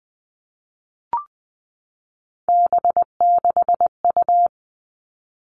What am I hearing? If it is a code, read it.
Morse code, heard as a keyed tone, about 20 wpm, 700 Hz. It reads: B6U
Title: Three Challenging Letter-Number Combinations 20wpm